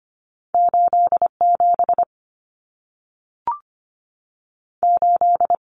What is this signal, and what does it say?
Morse code, heard as a keyed tone, about 25 wpm, 700 Hz. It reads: 87 8
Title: Common Cut Numbers - 2-Digit 25wpm